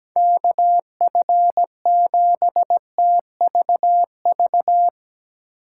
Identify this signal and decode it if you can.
Morse code, heard as a keyed tone, about 17 wpm, 700 Hz. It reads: KF7TVV
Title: USA Call Signs - 6 Characters - 2x Character Spacing - 17wpm